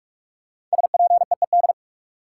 Morse code, heard as a keyed tone, about 46 wpm, 700 Hz. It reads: SPEED
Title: Sentences from Top 800 Words - New Word Drill - 45wpm